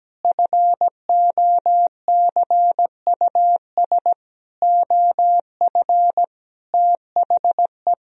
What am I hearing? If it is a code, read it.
Morse code, heard as a keyed tone, about 17 wpm, 700 Hz. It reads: FOCUS OF THE
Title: Sentences from Top 1000 Words - Review Part 4 - 17wpm